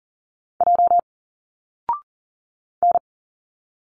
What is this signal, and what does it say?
Morse code, heard as a keyed tone, about 40 wpm, 700 Hz. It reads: J N
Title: Single Letters 40wpm